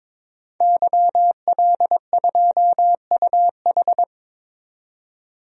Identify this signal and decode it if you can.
Morse code, heard as a keyed tone, about 22 wpm, 700 Hz. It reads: YL2UH
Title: DXCC-100 Call Signs - 5 Characters - 22wpm